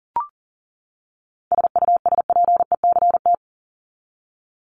Morse code, heard as a keyed tone, about 40 wpm, 700 Hz. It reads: SUSPECT
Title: Top 1751-2000 Words 40wpm